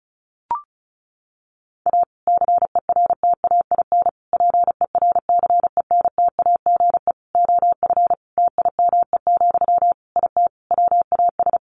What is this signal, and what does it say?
Morse code, heard as a keyed tone, about 34 wpm, 700 Hz. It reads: A CERTAIN PERCENTAGE OF TIME, IT WAS
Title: Sentences from Top 1300 Words 35wpm